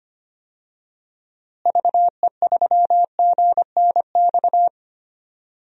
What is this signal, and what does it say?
Morse code, heard as a keyed tone, about 25 wpm, 700 Hz. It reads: VE3GNX
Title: DXCC-100 Call Signs - 6 Characters - Repeated 2x - 25wpm